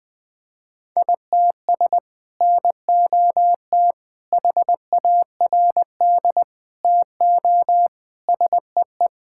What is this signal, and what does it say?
Morse code, heard as a keyed tone, about 20 wpm, 700 Hz. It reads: ITS NOT HARD TO SEE
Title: Sentences with 6-Word Phrases - Spoken Beginning - 20wpm